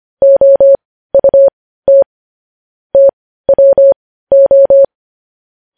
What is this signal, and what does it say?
Morse code, heard as a keyed tone, about 25 wpm, spaced as for 15 wpm, 550 Hz. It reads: OUT TWO
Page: news headlines in morse code at 15 wpm